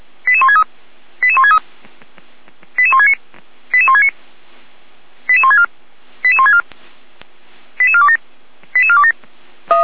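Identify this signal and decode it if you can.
Five-tone selective call (ZVEI), heard as quick runs of five tones, tones of about 70 ms. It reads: 88163 88163 88169 88169 88164 88164 88427 88427
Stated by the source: ZVEI five-tone selective call received on an SDR, recorded by the Feuerwehr Orlamuende fire brigade.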